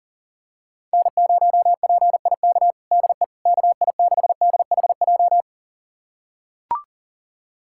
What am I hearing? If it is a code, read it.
Morse code, heard as a keyed tone, about 40 wpm, 700 Hz. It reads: N0PIK DE KI6DHJ
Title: USA Callsign DE Callsign 40wpm